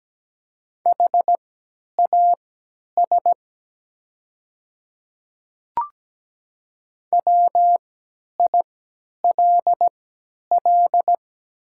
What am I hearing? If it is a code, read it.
Morse code, heard as a keyed tone, about 17 wpm, spaced as for 10 wpm, 700 Hz. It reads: HAS WILL
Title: 3x Character Spacing - Top 100 Words - 17wpm